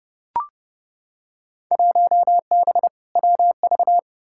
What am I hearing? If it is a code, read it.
Morse code, heard as a keyed tone, about 29 wpm, 700 Hz. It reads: 1B WV
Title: Field Day Exchanges Encore - 30wpm